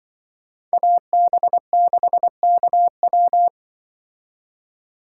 Morse code, heard as a keyed tone, about 24 wpm, 700 Hz. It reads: AB6KW
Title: Speed Racing - USA 2x2 Call Signs 20wpm